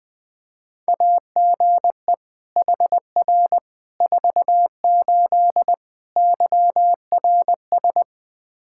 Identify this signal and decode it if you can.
Morse code, heard as a keyed tone, about 21 wpm, 700 Hz. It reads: AGE HR 48 YRS
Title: K1USN SST QSO Practice 20wpm